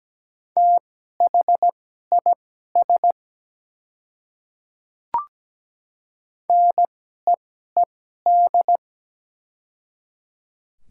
Morse code, heard as a keyed tone, about 17 wpm, spaced as for 12 wpm, 700 Hz. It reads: THIS NEED
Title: 2x Character Spacing - Top 200 Words - 17wpm